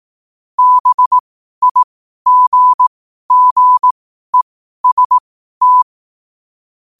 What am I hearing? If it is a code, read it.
Morse code, heard as a keyed tone, about 18 wpm, spaced as for 12 wpm, 1000 Hz. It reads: BIGGEST
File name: qotdfiles / 12wpm